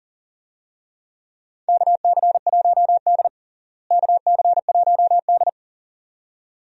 Morse code, heard as a keyed tone, about 37 wpm, 700 Hz. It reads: KC1D KC1D
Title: USA Call Signs - 4 Characters - Repeated 2x - 40wpm